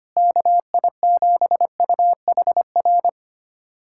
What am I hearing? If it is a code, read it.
Morse code, heard as a keyed tone, about 25 wpm, 700 Hz. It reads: KI7UHR
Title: USA Callsign DE Callsign - 6 Characters - 25wpm